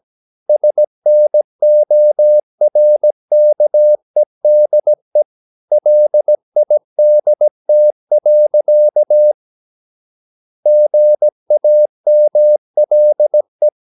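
Morse code, heard as a keyed tone, about 17 wpm, 600 Hz. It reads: SNORKEDE LIDT. GAMLE
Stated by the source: Slow (QRS) Morse practice at 17 wpm, from CW med Gnister.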